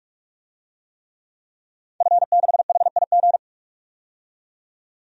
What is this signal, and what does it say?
Morse code, heard as a keyed tone, about 44 wpm, 700 Hz. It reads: F6HIG
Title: DXCC-100 Call Signs - 5 Characters - 45wpm